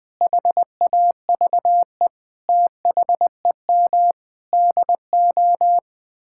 Morse code, heard as a keyed tone, about 20 wpm, 700 Hz. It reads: HAVE THEM DO
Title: Sets of 9 Words - 20wpm